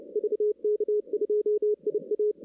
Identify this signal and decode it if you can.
Morse code, heard as a keyed tone, about 30 wpm, 420 Hz. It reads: VK2IA
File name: 2016 arrl dx cw / 2016-ARRL-DX-CW-K5TR- - 0-25